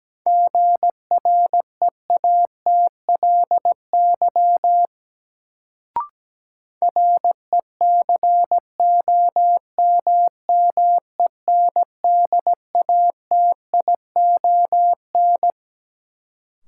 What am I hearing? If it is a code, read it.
Morse code, heard as a keyed tone, about 17 wpm, 700 Hz. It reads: GREATLY RECOMMENDATION
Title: Sentences from Top 1400 Words - New Word Drill - 17wpm